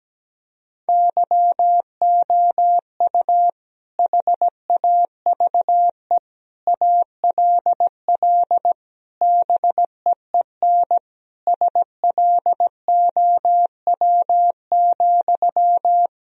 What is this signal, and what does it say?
Morse code, heard as a keyed tone, about 17 wpm, 700 Hz. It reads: YOU HAVE ALL BEEN SLOW,
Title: Sentences from Top 1000 Words - Review Part 2 - 17wpm